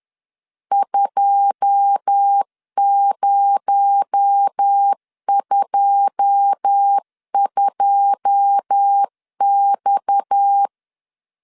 Morse code, heard as a keyed tone, about 10 wpm, 800 Hz. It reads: 2022X
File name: LEZIONI / 9 2 10wpm 2736